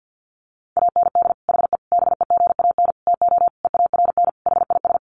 Morse code, heard as a keyed tone, about 50 wpm, 700 Hz. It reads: AND HE BEGAN TO EARN HIS
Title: Sentences from Top 1500 Words 50wpm